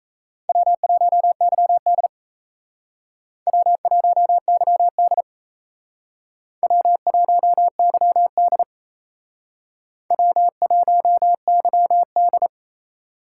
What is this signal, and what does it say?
Morse code, heard as a keyed tone, about 34 wpm, 700 Hz. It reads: W1YD W1YD W1YD W1YD
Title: Speed Racing - USA 1x2 Call Signs 28wpm